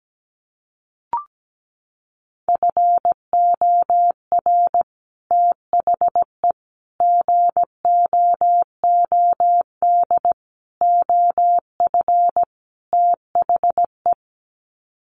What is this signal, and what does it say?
Morse code, heard as a keyed tone, about 17 wpm, 700 Hz. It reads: FOR THE GOOD OF THE
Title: Sets of 5 Words Encore2 17wpm